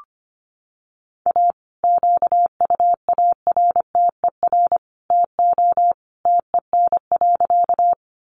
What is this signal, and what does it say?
Morse code, heard as a keyed tone, about 25 wpm, 700 Hz. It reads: A QUARTER TO TEN.
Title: Sentences from Top 1300 Words 25wpm